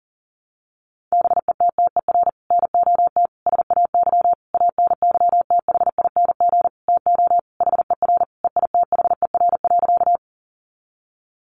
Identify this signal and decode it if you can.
Morse code, heard as a keyed tone, about 40 wpm, 700 Hz. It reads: BETTER NOT SAY ANYTHING TO HER EITHER.
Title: Sentences from Top 1000 Words 40wpm